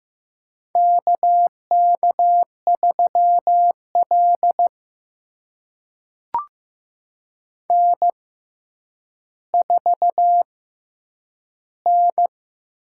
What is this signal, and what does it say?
Morse code, heard as a keyed tone, about 15 wpm, spaced as for 9 wpm, 700 Hz. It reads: KK3L N4N
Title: USA Call Signs - 4 Characters - 6x Character Spacing - 15wpm